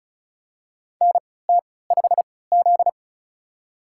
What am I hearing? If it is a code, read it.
Morse code, heard as a keyed tone, about 34 wpm, spaced as for 20 wpm, 700 Hz. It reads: NT5Z
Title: USA Call Signs - 4 Characters - 3x Character Spacing - 35wpm